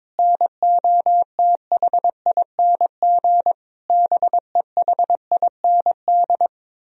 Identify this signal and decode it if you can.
Morse code, heard as a keyed tone, about 22 wpm, 700 Hz. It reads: NOTHING BEHIND
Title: Sentences from Top 1000 Words - Review Part 1 - 22wpm